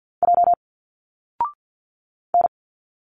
Morse code, heard as a keyed tone, about 46 wpm, 700 Hz. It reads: J N
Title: Single Letters 50wpm